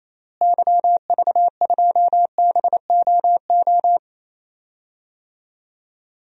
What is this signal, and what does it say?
Morse code, heard as a keyed tone, about 27 wpm, 700 Hz. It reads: YV2BOO
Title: DXCC-100 Call Signs - 6 Characters - Repeated 2x - 28wpm